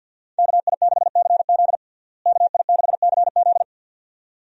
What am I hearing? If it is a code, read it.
Morse code, heard as a keyed tone, about 47 wpm, 700 Hz. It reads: KI6CZ KI6CZ
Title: USA Call Signs - 5 Characters - Repeated 2x - 50wpm